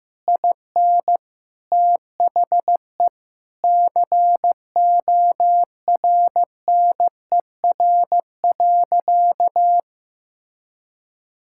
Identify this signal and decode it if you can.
Morse code, heard as a keyed tone, about 15 wpm, 700 Hz. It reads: IN THE CORNER.
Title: Sentences from Top 1600 Words 15wpm